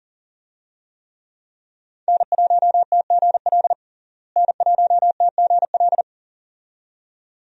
Morse code, heard as a keyed tone, about 38 wpm, 700 Hz. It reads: N1TGL N1TGL
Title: USA Call Signs - 5 Characters - Repeated 2x - 40wpm